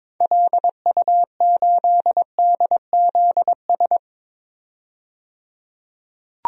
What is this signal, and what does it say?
Morse code, heard as a keyed tone, about 22 wpm, 700 Hz. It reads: LU8DZS
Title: DXCC-100 Call Signs - 6 Characters - 22wpm